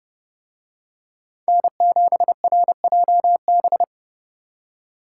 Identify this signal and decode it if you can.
Morse code, heard as a keyed tone, about 30 wpm, 700 Hz. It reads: N7RJB